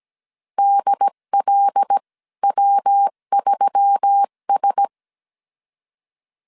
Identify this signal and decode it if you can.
Morse code, heard as a keyed tone, about 17 wpm, spaced as for 14 wpm, 800 Hz. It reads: DLW3S
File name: CW / DET / 1 6 1500C S C 16wpm 2733